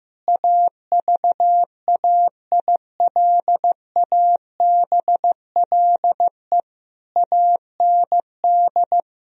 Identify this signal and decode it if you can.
Morse code, heard as a keyed tone, about 15 wpm, 700 Hz. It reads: AVAILABLE AND
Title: Sentences from Top 1000 Words - Review Part 3 - 15wpm